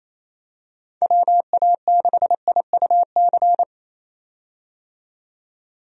Morse code, heard as a keyed tone, about 28 wpm, 700 Hz. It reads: WA6IUC
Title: USA Call Signs - 6 Characters - 28wpm